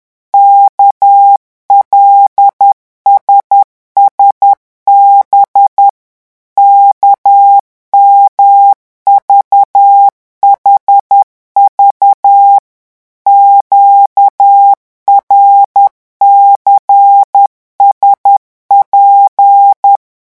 Morse code, heard as a keyed tone, about 10 wpm, 800 Hz. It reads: KLSSB KMVHV QRCSP
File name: morse mp3 / revision5-vitesse 10 mots